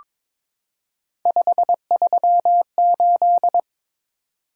Morse code, heard as a keyed tone, about 22 wpm, 700 Hz. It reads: 538
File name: practice / 22wpm / Three Letter-Number Combinations - 22wpm